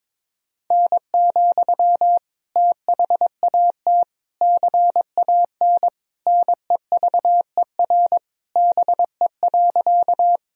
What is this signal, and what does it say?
Morse code, heard as a keyed tone, about 22 wpm, 700 Hz. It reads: N, THAT CAN NEVER BE.
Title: Sentences from Top 300 Words and Abbreviations 22wpm